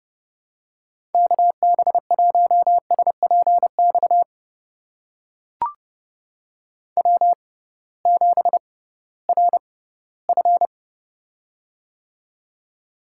Morse code, heard as a keyed tone, about 30 wpm, spaced as for 18 wpm, 700 Hz. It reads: KB1SPX W7RF
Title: USA Call Signs - 6x Character Spacing - 30wpm